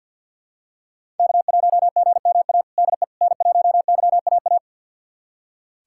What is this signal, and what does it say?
Morse code, heard as a keyed tone, about 51 wpm, 690 Hz. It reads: K1GMA DE N1YAA
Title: USA Callsign DE Callsign - 5 Characters - 50wpm